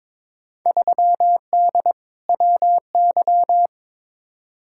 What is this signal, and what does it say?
Morse code, heard as a keyed tone, about 21 wpm, 700 Hz. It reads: 3D WY
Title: Field Day Exchanges Encore - 22wpm